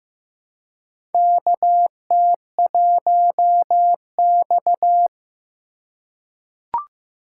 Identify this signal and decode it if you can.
Morse code, heard as a keyed tone, about 16 wpm, 700 Hz. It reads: KT1X
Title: USA Call Signs - 4 Characters - 2x Character Spacing - 15wpm